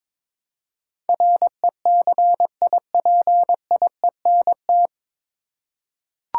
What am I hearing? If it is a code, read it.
Morse code, heard as a keyed tone, about 22 wpm, 700 Hz. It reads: RECIPIENT
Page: Morse Code Ninja practice 22wpm